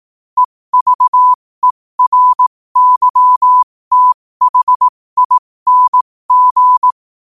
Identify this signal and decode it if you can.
Morse code, heard as a keyed tone, about 18 wpm, spaced as for 15 wpm, 1000 Hz. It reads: EVERYTHING
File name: qotdfiles / 15wpm